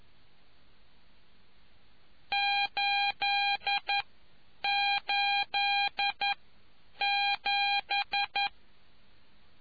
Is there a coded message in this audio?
Yes: Morse code from 2.3 to 8.5 s, heard as a keyed tone, about 11 wpm, spaced as for 8 wpm, 800 Hz. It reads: 887